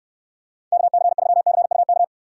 Morse code, heard as a keyed tone, about 66 wpm, 700 Hz. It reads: KC3CUK